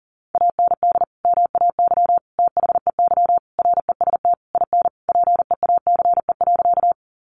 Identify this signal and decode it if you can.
Morse code, heard as a keyed tone, about 40 wpm, 700 Hz. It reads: AND MAY THEY REST IN PEACE.
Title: Sentences from Top 1500 Words 40wpm